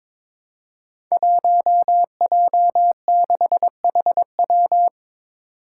Morse code, heard as a keyed tone, about 22 wpm, 700 Hz. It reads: 1J6HW
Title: Five Challenging Letter-Number Combinations 22wpm